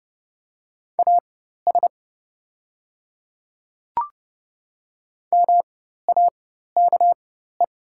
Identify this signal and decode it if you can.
Morse code, heard as a keyed tone, about 30 wpm, spaced as for 14 wpm, 690 Hz. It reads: AS MAKE
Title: 4x Character Spacing - Top 100 Words - 30wpm